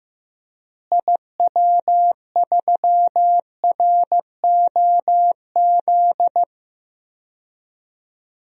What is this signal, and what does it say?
Morse code, heard as a keyed tone, about 15 wpm, 700 Hz. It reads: IW3ROZ
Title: DXCC-100 Call Signs - 6 Characters - Repeated 2x - 15wpm